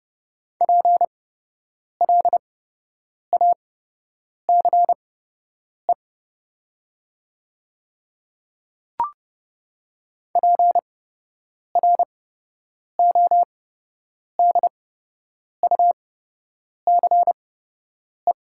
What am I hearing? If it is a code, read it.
Morse code, heard as a keyed tone, about 30 wpm, spaced as for 8 wpm, 700 Hz. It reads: PLACE PRODUCE
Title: 8x Character Spacing - Top 500 Words - 30wpm